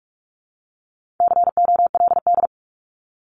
Morse code, heard as a keyed tone, about 44 wpm, 700 Hz. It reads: COLD
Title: Top 500 Words 45wpm